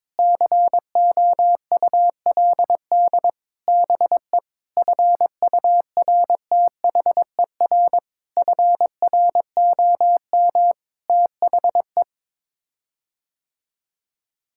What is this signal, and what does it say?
Morse code, heard as a keyed tone, about 22 wpm, 700 Hz. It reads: COULD BE FURTHER FROM THE
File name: practice / 22wpm / Sets of 6 Words - 22wpm